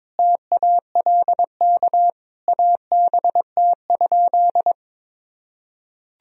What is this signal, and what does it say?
Morse code, heard as a keyed tone, about 21 wpm, 700 Hz. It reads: TALK ABT?
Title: Sentences from Top 400 Words and Abbreviations 22wpm